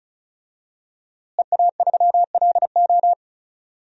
Morse code, heard as a keyed tone, about 36 wpm, 700 Hz. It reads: EA3LO
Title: DXCC-100 Call Signs - 5 Characters - Repeated 3x - 35wpm